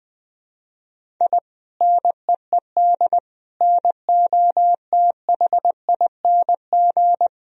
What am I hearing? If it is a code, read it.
Morse code, heard as a keyed tone, about 20 wpm, 700 Hz. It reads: I NEED NOTHING
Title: Sentences from Top 1000 Words - Review Part 1 - 20wpm